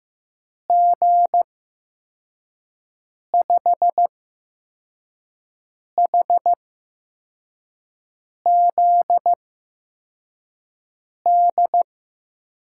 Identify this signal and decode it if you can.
Morse code, heard as a keyed tone, about 15 wpm, spaced as for 4 wpm, 700 Hz. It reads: G5HZD
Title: USA Call Signs - 6 Characters - 8x Character Spacing - 15wpm